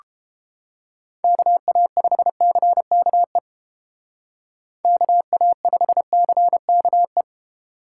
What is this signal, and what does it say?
Morse code, heard as a keyed tone, about 30 wpm, 700 Hz. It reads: KA5CKE KA5CKE